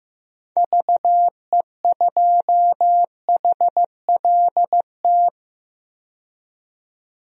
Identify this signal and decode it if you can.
Morse code, heard as a keyed tone, about 15 wpm, 700 Hz. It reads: VE2HLT
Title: DXCC-100 Call Signs - 6 Characters - 15wpm